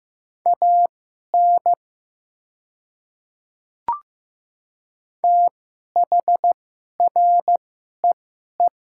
Morse code, heard as a keyed tone, about 15 wpm, spaced as for 11 wpm, 690 Hz. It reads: AN THREE
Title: 2x Character Spacing - Top 500 Words - 15wpm